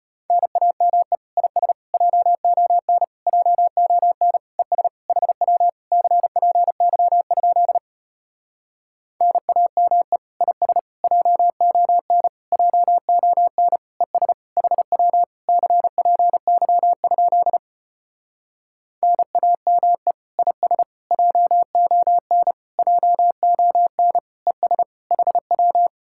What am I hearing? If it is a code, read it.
Morse code, heard as a keyed tone, about 33 wpm, 700 Hz. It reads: NAME IS JON JON ES HW CPY? NAME IS JON JON ES HW CPY? NAME IS JON JON ES HW